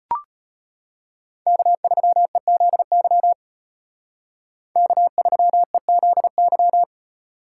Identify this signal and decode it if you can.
Morse code, heard as a keyed tone, about 31 wpm, 700 Hz. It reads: K3EZY K3EZY
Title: Speed Racing - USA 1x3 Call Signs 25wpm